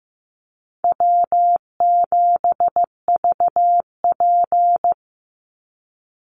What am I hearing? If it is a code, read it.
Morse code, heard as a keyed tone, about 15 wpm, 700 Hz. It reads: W7VP
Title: Call Signs Repeated 2x - 15wpm